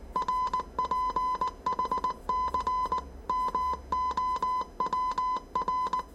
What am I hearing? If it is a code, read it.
Morse code, heard as a keyed tone, about 19 wpm, 1010 Hz. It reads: RPHC MOWR
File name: 业余无线电 / ●未分类 / 字码80